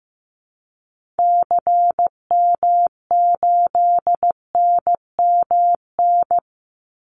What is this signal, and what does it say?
Morse code, heard as a keyed tone, about 15 wpm, 700 Hz. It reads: CM8NMN
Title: Call Signs Repeated 2x - 15wpm